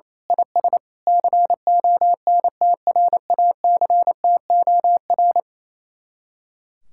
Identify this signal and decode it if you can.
Morse code, heard as a keyed tone, about 26 wpm, 700 Hz. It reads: IS CONTRACTOR
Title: Speed-Racing Common QSO Phrases - 28wpm